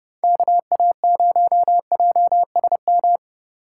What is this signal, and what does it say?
Morse code, heard as a keyed tone, about 30 wpm, 700 Hz. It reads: KA0JSM